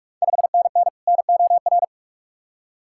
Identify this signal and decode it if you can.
Morse code, heard as a keyed tone, about 45 wpm, 690 Hz. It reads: 5NN NOR
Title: Ontario QSO Party Exchange 45wpm